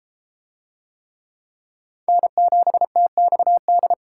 Morse code, heard as a keyed tone, about 30 wpm, 700 Hz. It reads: N7TXD